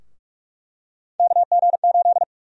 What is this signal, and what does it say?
Morse code, heard as a keyed tone, about 47 wpm, 700 Hz. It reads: KG8
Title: USA Call Sign Prefixes - Repeated 2x - 45wpm